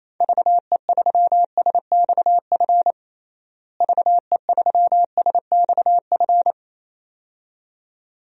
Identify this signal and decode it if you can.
Morse code, heard as a keyed tone, about 27 wpm, 700 Hz. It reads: VE3SXF VE3SXF
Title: DXCC-100 Call Signs - 6 Characters - Repeated 2x - 28wpm